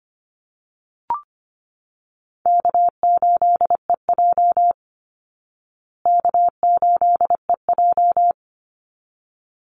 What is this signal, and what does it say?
Morse code, heard as a keyed tone, about 25 wpm, 700 Hz. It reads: K8EJ K8EJ
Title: Call Signs Repeated 3x - 25wpm